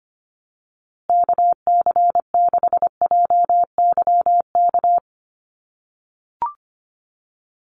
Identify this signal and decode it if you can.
Morse code, heard as a keyed tone, about 25 wpm, 700 Hz. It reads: KC6JYK
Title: USA Call Signs Repeated 3x 25wpm